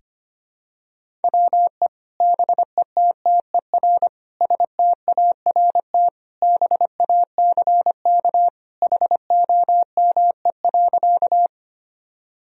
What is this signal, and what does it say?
Morse code, heard as a keyed tone, about 25 wpm, 700 Hz. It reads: WE BETTER START BACK HOME.
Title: Sentences from Top 1000 Words - Review Part 1 - 25wpm